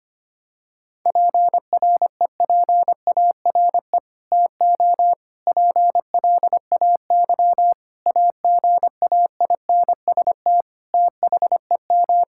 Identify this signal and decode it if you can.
Morse code, heard as a keyed tone, about 25 wpm, 700 Hz. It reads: PREPARE TO PLAY AGAINST THEM
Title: Sentences from Top 1000 Words - Review Part 6 - 25wpm